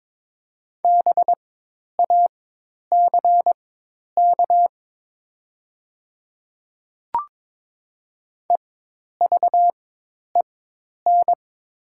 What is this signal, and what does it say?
Morse code, heard as a keyed tone, about 22 wpm, spaced as for 10 wpm, 700 Hz. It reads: BACK EVEN
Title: 4x Character Spacing - Top 300 Words - 22wpm